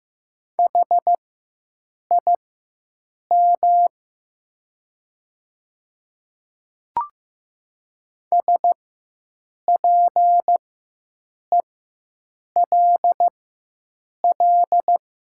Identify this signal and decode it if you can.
Morse code, heard as a keyed tone, about 15 wpm, spaced as for 7 wpm, 700 Hz. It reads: HIM SPELL
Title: 4x Character Spacing - Top 200 Words - 15wpm